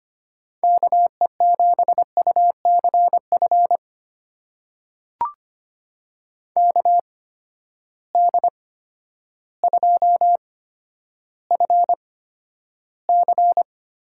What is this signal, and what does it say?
Morse code, heard as a keyed tone, about 25 wpm, spaced as for 11 wpm, 700 Hz. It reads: KE7UCF KD2FC
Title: USA Call Signs - 6 Characters - 8x Character Spacing - 25wpm